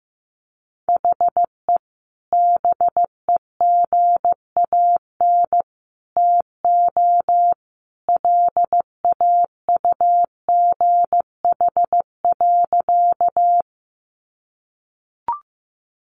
Morse code, heard as a keyed tone, about 15 wpm, 700 Hz. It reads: HE BEGAN TO LAUGH.
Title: Sentences from Top 500 Words 15wpm